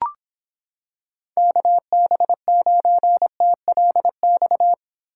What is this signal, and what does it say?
Morse code, heard as a keyed tone, about 26 wpm, 700 Hz. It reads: KB9TLX